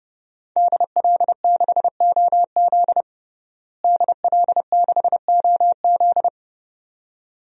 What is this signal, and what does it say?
Morse code, heard as a keyed tone, about 32 wpm, 700 Hz. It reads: DL6OZ DL6OZ
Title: DXCC-100 Call Signs - 5 Characters - Repeated 2x - 30wpm